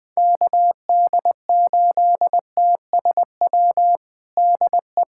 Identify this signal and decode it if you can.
Morse code, heard as a keyed tone, about 20 wpm, 700 Hz. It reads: KD8TSW DE
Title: USA Callsign DE Callsign - 6 Characters - 20wpm